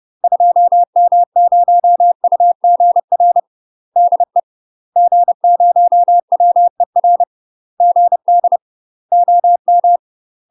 Morse code, heard as a keyed tone, about 30 wpm, 700 Hz. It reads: 2M0UGR DE G0WER GD OM
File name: UK MASTER QSOs TRAINING/30WPM